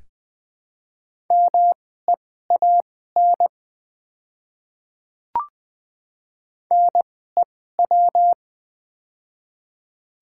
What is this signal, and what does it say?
Morse code, heard as a keyed tone, about 20 wpm, spaced as for 14 wpm, 700 Hz. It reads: MEAN NEW
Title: 2x Character Spacing - Top 200 Words - 20wpm